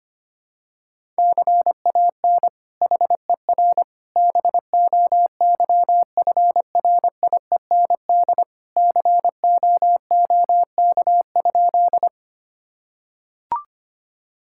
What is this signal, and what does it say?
Morse code, heard as a keyed tone, about 25 wpm, 700 Hz. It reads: CAN HER BOYFRIEND COOK?
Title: Sentences from Top 1000 Words - Review Part 5 - 25wpm